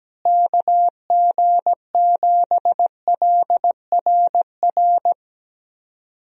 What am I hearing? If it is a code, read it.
Morse code, heard as a keyed tone, about 17 wpm, 700 Hz. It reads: KG7LRR
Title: USA Call Signs - 6 Characters - Repeated 3x - 17wpm